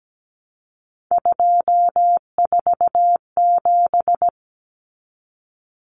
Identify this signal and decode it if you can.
Morse code, heard as a keyed tone, about 17 wpm, 700 Hz. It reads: 247
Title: Numbers - 3-Digit 17wpm